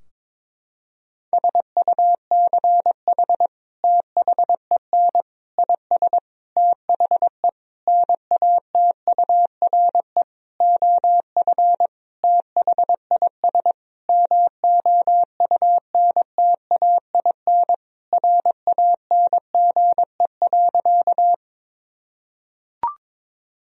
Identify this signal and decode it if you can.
Morse code, heard as a keyed tone, about 22 wpm, 700 Hz. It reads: SUCH THEN IS THE NATURE OF THIS MOUNTAIN RANGE.
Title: Sentences from Top 1000 Words - Review Part 4 - 22wpm